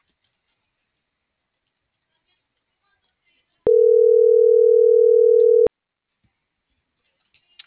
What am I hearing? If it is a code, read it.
Ringback tone (North America), 440 + 480 Hz.